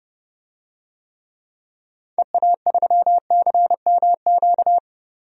Morse code, heard as a keyed tone, about 32 wpm, 700 Hz. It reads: EA3CMQ
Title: DXCC-100 Call Signs - 6 Characters - 30wpm